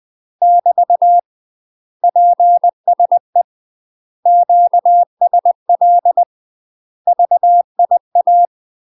Morse code, heard as a keyed tone, about 20 wpm, 700 Hz. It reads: <BT> PSE QSL VIA